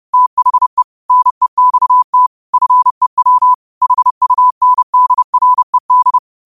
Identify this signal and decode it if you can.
Morse code, heard as a keyed tone, about 30 wpm, 1000 Hz. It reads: THE NEXT FEW HUNDRED
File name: qotdfiles / 30wpm